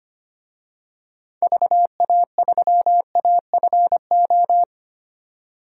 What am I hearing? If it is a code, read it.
Morse code, heard as a keyed tone, about 25 wpm, 700 Hz. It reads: VA3AFO
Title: DXCC-100 Call Signs - 6 Characters - 25wpm